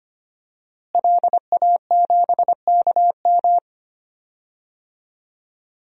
Morse code, heard as a keyed tone, about 25 wpm, 700 Hz. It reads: LA7KM
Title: DXCC-100 Call Signs - 5 Characters - Repeated 2x - 25wpm